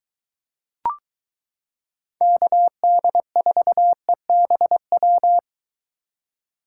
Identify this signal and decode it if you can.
Morse code, heard as a keyed tone, about 22 wpm, 700 Hz. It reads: KD4EBW